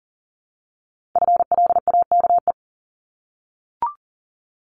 Morse code, heard as a keyed tone, about 42 wpm, 700 Hz. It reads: FLAKE
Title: Multi-letter Phonemes 40wpm